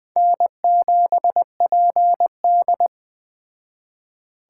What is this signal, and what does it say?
Morse code, heard as a keyed tone, about 20 wpm, 700 Hz. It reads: N7PD
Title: Speed Racing - USA 1x2 Call Signs 17wpm